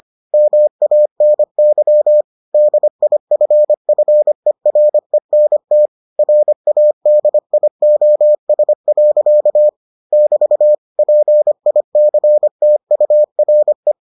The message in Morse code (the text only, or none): MANY DIFFERENT RADIOS. <BT> PICTURE